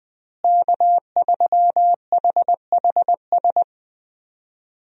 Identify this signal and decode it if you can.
Morse code, heard as a keyed tone, about 20 wpm, 700 Hz. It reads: K3HHS
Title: USA Call Signs - 5 Characters - Repeated 3x - 20wpm